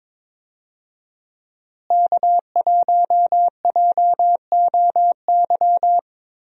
Morse code, heard as a keyed tone, about 22 wpm, 700 Hz. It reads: K1JOY
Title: USA Call Signs - 5 Characters - Repeated 2x - 22wpm